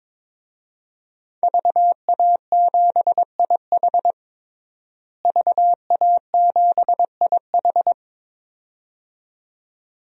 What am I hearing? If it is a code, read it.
Morse code, heard as a keyed tone, about 21 wpm, 700 Hz. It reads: VA7IH VA7IH
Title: DXCC-100 Call Signs - 5 Characters - Repeated 2x - 22wpm